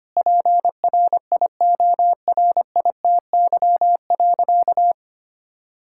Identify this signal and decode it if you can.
Morse code, heard as a keyed tone, about 25 wpm, 700 Hz. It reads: PRIORITY.
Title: Sentences from Top 1600 Words 25wpm